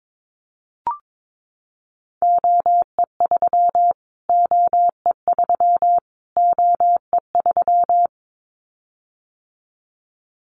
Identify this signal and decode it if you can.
Morse code, heard as a keyed tone, about 22 wpm, 700 Hz. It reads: OE3 OE3 OE3
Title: Callsign Prefixes for Top 25 DXCC - Repeated 3x - 22wpm